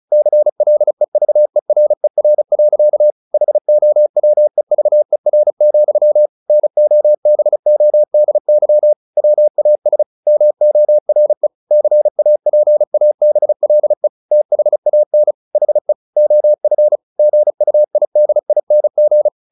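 Morse code, heard as a keyed tone, about 34 wpm, 600 Hz. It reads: CLEVERER. HOWEVER, NOBODY WAS MORE CAPABLE THAN HE OF GUIDING
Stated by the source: Morse practice at 35 wpm from CW med Gnister, sending Ken Follett's novel Never in English.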